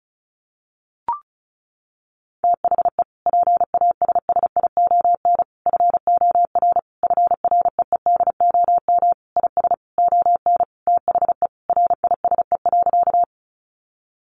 Morse code, heard as a keyed tone, about 34 wpm, 700 Hz. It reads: THE PASSION FOR FREEDOM IS ON THE RISE.
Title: Sentences from Top 1300 Words 35wpm